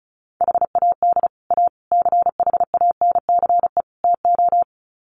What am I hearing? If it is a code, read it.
Morse code, heard as a keyed tone, about 34 wpm, 700 Hz. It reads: HAD A CHANCE TO
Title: Sets of 4 Words 35wpm